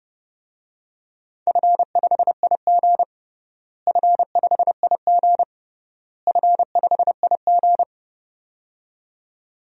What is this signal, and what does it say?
Morse code, heard as a keyed tone, about 30 wpm, 700 Hz. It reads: F5IG F5IG F5IG
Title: DXCC-100 Call Signs - 4 Characters - Repeated 3x - 30wpm